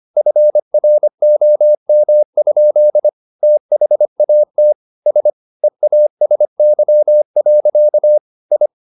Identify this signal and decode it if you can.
Morse code, heard as a keyed tone, about 25 wpm, 600 Hz. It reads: FROM? THAT S EASY. I
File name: the old man and the sea part 1 25wpm